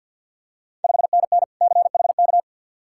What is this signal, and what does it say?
Morse code, heard as a keyed tone, about 51 wpm, 690 Hz. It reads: 5NN CHK
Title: Ontario QSO Party Exchange 50wpm